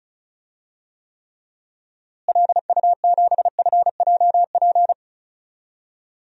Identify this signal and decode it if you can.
Morse code, heard as a keyed tone, about 36 wpm, 700 Hz. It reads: LU7FJP
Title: DXCC-100 Call Signs - 6 Characters - Repeated 3x - 35wpm